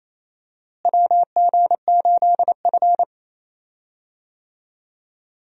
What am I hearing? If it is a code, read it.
Morse code, heard as a keyed tone, about 28 wpm, 700 Hz. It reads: WG8F
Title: USA Call Signs - 4 Characters - Repeated 2x - 28wpm